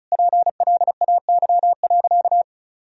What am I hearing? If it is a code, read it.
Morse code, heard as a keyed tone, about 36 wpm, 700 Hz. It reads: PLAY.
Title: Sentences from Top 1000 Words - Review Part 1 - 35wpm